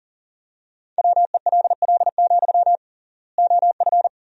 Morse code, heard as a keyed tone, about 40 wpm, 700 Hz. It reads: WELL, OF
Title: 3x Word-Spacing - Sentences from Top 500 Words - 40wpm